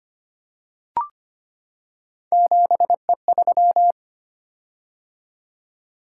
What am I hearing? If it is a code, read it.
Morse code, heard as a keyed tone, about 25 wpm, 700 Hz. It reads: 7E3
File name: Three Letter-Number Combinations - 25wpm